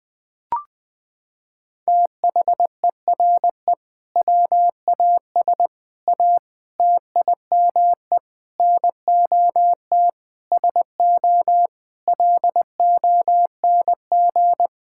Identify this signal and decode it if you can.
Morse code, heard as a keyed tone, about 21 wpm, 700 Hz. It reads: THERE WAS A TIME NOT SO LONG
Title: Sets of 8 Words - 20wpm